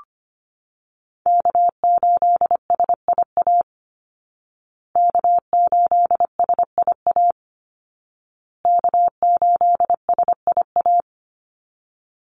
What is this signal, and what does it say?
Morse code, heard as a keyed tone, about 25 wpm, 700 Hz. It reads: K8SIA K8SIA K8SIA
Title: Call Signs Repeated 2x - 25wpm